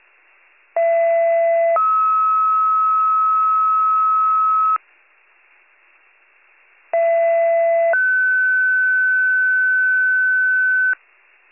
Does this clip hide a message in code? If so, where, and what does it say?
Yes: two-tone page from 0.8 to 10.9 s, heard as a one-second tone, then a three-second tone, about 670 then 1290 Hz, about 670 then 1500 Hz.